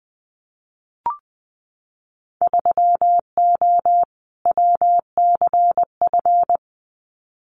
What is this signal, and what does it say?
Morse code, heard as a keyed tone, about 20 wpm, 700 Hz. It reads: 3O WCF
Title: Winter Field Day Exchanges 20wpm